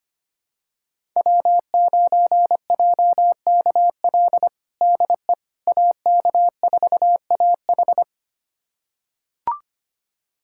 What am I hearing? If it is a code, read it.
Morse code, heard as a keyed tone, about 25 wpm, 700 Hz. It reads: W9JKL DE AK4AH
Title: USA Callsign DE Callsign 25wpm